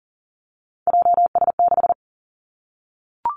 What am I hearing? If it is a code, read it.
Morse code, heard as a keyed tone, about 37 wpm, 700 Hz. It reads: JS6
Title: Three Challenging Letter-Number Combinations 40wpm